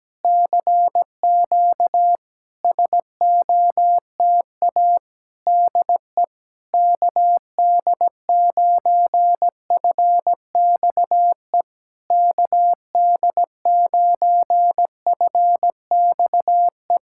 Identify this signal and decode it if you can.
Morse code, heard as a keyed tone, about 17 wpm, 700 Hz. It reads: CQ SOTA DE KD9FXE KD9FXE
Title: SOTA QSO Practice 17wpm